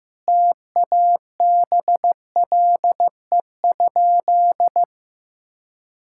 Morse code, heard as a keyed tone, about 15 wpm, 700 Hz. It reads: TABLE?
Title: Sentences from Top 1600 Words 15wpm